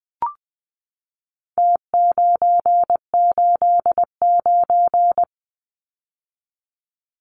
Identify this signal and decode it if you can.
Morse code, heard as a keyed tone, about 20 wpm, 700 Hz. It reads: T989
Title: Numbers - Contest Style - 4-Digit 20wpm